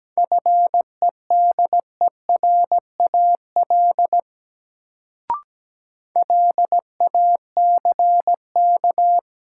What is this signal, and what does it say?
Morse code, heard as a keyed tone, about 17 wpm, 700 Hz. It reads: FEDERAL LACK
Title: Sets of 3 Words - New Word Drill - 17wpm